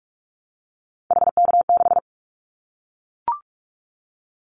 Morse code, heard as a keyed tone, about 44 wpm, 700 Hz. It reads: HK6
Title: Callsign Prefixes for Top 100 DXCC - 45wpm